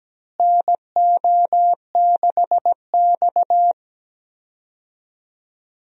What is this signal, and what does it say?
Morse code, heard as a keyed tone, about 17 wpm, 700 Hz. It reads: NO6X
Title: USA Call Signs - 4 Characters - Repeated 3x - 17wpm